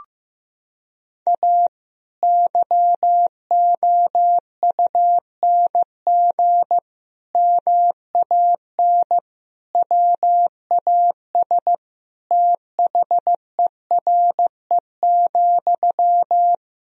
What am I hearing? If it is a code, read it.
Morse code, heard as a keyed tone, about 15 wpm, 700 Hz. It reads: A YOUNG MAN WAS THERE,
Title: Sentences from Top 1000 Words - Review Part 3 - 15wpm